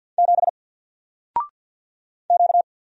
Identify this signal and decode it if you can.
Morse code, heard as a keyed tone, about 48 wpm, 700 Hz. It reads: 7 Q
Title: Single Letter-Number Alt 50wpm